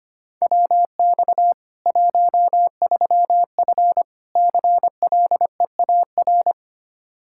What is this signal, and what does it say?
Morse code, heard as a keyed tone, about 25 wpm, 700 Hz. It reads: WX 13F CLEAR
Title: K1USN SST QSO Practice 25wpm